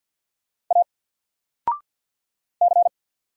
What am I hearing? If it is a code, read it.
Morse code, heard as a keyed tone, about 50 wpm, 700 Hz. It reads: A C
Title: Single Letter-Number 50wpm